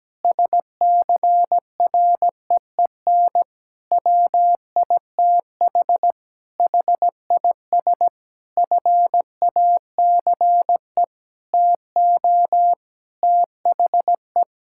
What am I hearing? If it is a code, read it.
Morse code, heard as a keyed tone, about 17 wpm, 700 Hz. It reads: SCREEN WITH HIS FACE TO THE
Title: Sentences from Top 1000 Words - Review Part 6 - 17wpm